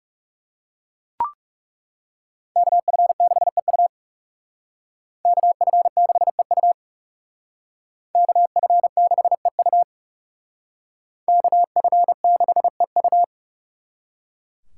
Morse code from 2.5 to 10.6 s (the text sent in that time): KF6EU KF6EU KF6EU